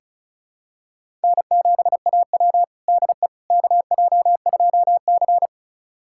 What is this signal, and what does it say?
Morse code, heard as a keyed tone, about 36 wpm, 700 Hz. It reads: N7AW DE KJ2C
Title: USA Callsign DE Callsign - 4 Characters - 35wpm